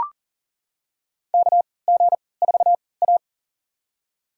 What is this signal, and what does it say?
Morse code, heard as a keyed tone, about 40 wpm, spaced as for 23 wpm, 700 Hz. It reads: KG4A
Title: USA Call Signs - 4 Characters - 3x Character Spacing - 40wpm